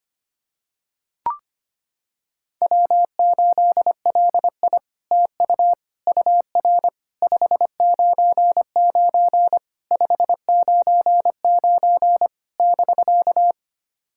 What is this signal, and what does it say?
Morse code, heard as a keyed tone, about 25 wpm, 700 Hz. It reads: W8LI TU UR 599 599 <BK>
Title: SOTA QSO Practice 25wpm